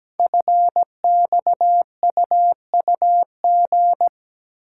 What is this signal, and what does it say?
Morse code, heard as a keyed tone, about 17 wpm, 700 Hz. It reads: FXUUG
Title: Code Groups - 5 Letters - 17wpm